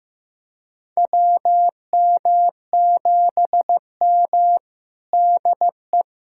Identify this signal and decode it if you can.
Morse code, heard as a keyed tone, about 16 wpm, 700 Hz. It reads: WM7M DE
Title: USA Callsign DE Callsign - 4 Characters - 15wpm